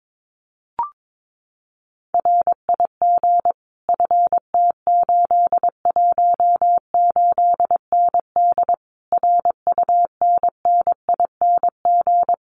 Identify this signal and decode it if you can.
Morse code, heard as a keyed tone, about 22 wpm, 700 Hz. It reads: RIG FT818ND RUNNING
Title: Common QSO Phrases Encore 22wpm